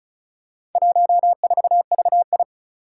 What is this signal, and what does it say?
Morse code, heard as a keyed tone, about 36 wpm, 700 Hz. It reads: 14VI
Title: Four Challenging Letter-Number Combinations 35wpm